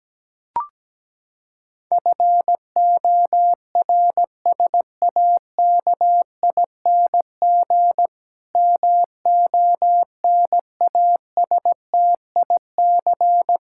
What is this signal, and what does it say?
Morse code, heard as a keyed tone, about 17 wpm, 700 Hz. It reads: FORSAKING MONASTIC
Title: Pangram Challenge v2 - 17wpm